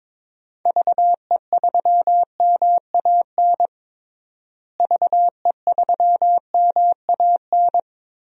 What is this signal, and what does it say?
Morse code, heard as a keyed tone, about 22 wpm, 700 Hz. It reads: VE3MAN VE3MAN
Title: DXCC-100 Call Signs - 6 Characters - Repeated 2x - 22wpm